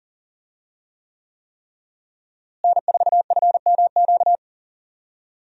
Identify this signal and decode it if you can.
Morse code, heard as a keyed tone, about 40 wpm, 700 Hz. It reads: N4FMQ
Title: USA Call Signs - 5 Characters - Repeated 3x - 40wpm